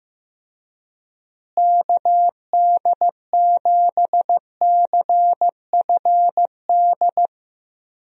Morse code, heard as a keyed tone, about 15 wpm, 700 Hz. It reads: KD7CFD
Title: USA Call Signs - 6 Characters - Repeated 3x - 15wpm